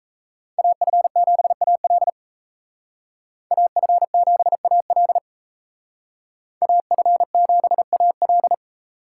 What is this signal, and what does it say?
Morse code, heard as a keyed tone, about 33 wpm, 700 Hz. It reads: AF7AL AF7AL AF7AL
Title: Speed Racing - USA 2x2 Call Signs 28wpm